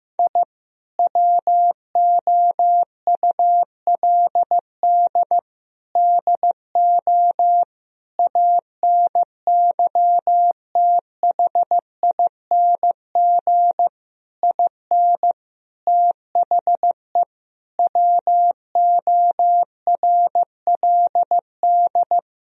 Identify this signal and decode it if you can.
Morse code, heard as a keyed tone, about 15 wpm, 700 Hz. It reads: I WOULD DO ANYTHING IN THE WORLD
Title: Sentences from Top 1000 Words - Review Part 2 - 15wpm